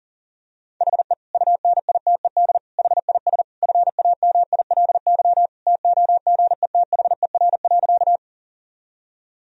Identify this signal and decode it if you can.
Morse code, heard as a keyed tone, about 40 wpm, 700 Hz. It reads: HE UNITED HIS FAMILY TOGETHER.
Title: Sentences from Top 1000 Words - Review Part 4 - 40wpm